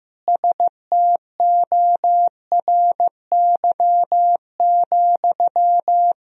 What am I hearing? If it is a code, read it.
Morse code, heard as a keyed tone, about 15 wpm, 700 Hz. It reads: STORY,
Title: Sentences from Top 1000 Words - Review Part 3 - 15wpm